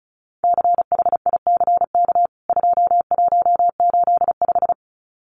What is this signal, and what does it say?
Morse code, heard as a keyed tone, about 34 wpm, 700 Hz. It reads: CHICK 2185
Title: CWT Exchange 35wpm